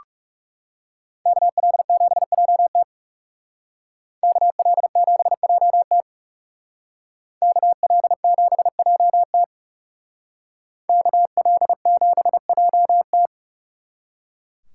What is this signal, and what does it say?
Morse code, heard as a keyed tone, about 39 wpm, 700 Hz. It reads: KL7JT KL7JT KL7JT KL7JT